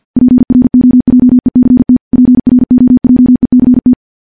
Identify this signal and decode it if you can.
Morse code, heard as a keyed tone, about 50 wpm, 250 Hz. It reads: PROJECT PROJECT
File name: morse-training-50WPM